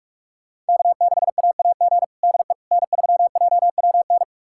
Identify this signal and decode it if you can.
Morse code, heard as a keyed tone, about 46 wpm, 700 Hz. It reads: K6AAG DE N3JWN
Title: USA Callsign DE Callsign - 5 Characters - 45wpm